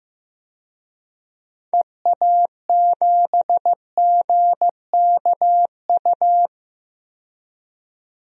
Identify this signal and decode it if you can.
Morse code, heard as a keyed tone, about 15 wpm, 700 Hz. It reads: EA7GKU
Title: DXCC-100 Call Signs - 6 Characters - 15wpm